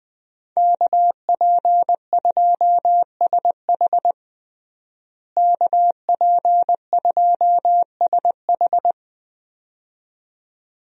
Morse code, heard as a keyed tone, about 21 wpm, 700 Hz. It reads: KP2SH KP2SH
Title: DXCC-100 Call Signs - 5 Characters - Repeated 2x - 20wpm